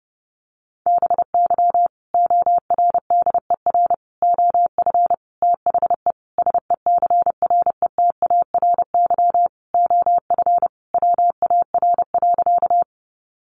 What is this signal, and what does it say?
Morse code, heard as a keyed tone, about 28 wpm, 700 Hz. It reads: BY ORDER OF THE SECRETARY OF WAR.
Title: Sentences from Top 1300 Words 30wpm